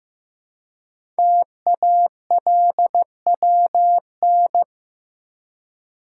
Morse code, heard as a keyed tone, about 15 wpm, 700 Hz. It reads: TALWN
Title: Code Groups - 5 Letters - 15wpm